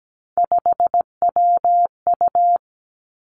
Morse code, heard as a keyed tone, about 17 wpm, 700 Hz. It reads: 5WU
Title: Three Challenging Letter-Number Combinations 17wpm